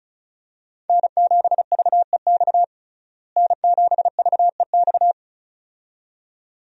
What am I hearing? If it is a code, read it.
Morse code, heard as a keyed tone, about 36 wpm, 700 Hz. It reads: N7VEX N7VEX
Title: USA Call Signs - 5 Characters - Repeated 2x - 35wpm